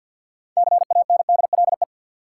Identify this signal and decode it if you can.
Morse code, heard as a keyed tone, about 47 wpm, 700 Hz. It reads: CANDLE
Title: Six Letter Words - 50wpm